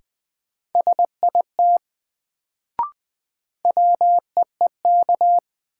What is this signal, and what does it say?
Morse code, heard as a keyed tone, about 20 wpm, 700 Hz. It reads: SIT WEEK
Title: Sentences from Top 500 Words - New Word Drill - 20wpm